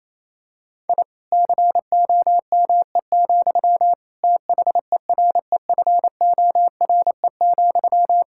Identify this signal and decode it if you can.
Morse code, heard as a keyed tone, about 28 wpm, 700 Hz. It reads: I COME, THEREFORE,
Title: Sentences from Top 1000 Words - Review Part 5 - 28wpm